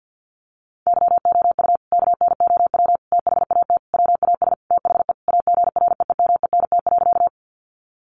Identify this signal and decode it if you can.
Morse code, heard as a keyed tone, about 50 wpm, 700 Hz. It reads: YOU KNOW THAT WAS THE AGREEMENT.
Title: Sentences from Top 1500 Words 50wpm